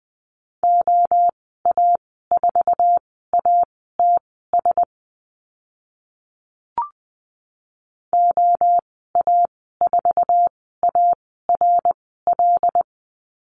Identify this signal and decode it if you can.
Morse code, heard as a keyed tone, about 20 wpm, spaced as for 14 wpm, 700 Hz. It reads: OA4ATS OA4ARL
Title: 082 Lesson 14g - Number 2 - Call Signs 20wpm